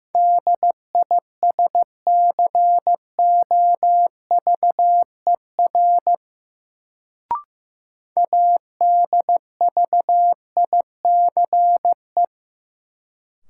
Morse code, heard as a keyed tone, about 15 wpm, 700 Hz. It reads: DISCOVER ADVICE
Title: Sentences from Top 1100 Words - New Word Drill Alt - 15wpm